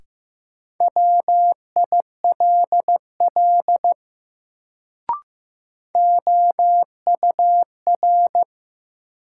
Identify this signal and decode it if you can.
Morse code, heard as a keyed tone, about 15 wpm, 700 Hz. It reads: WILL OUR
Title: Sets of 2 Words - New Word Drill - 15wpm